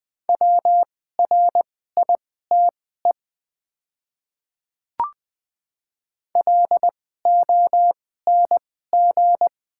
Morse code, heard as a keyed tone, about 20 wpm, spaced as for 15 wpm, 700 Hz. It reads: WRITE LONG
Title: 2x Character Spacing - Top 100 Words - 20wpm